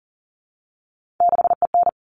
Morse code, heard as a keyed tone, about 40 wpm, 700 Hz. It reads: 6EN
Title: All Cut Numbers - 3-Digit 40wpm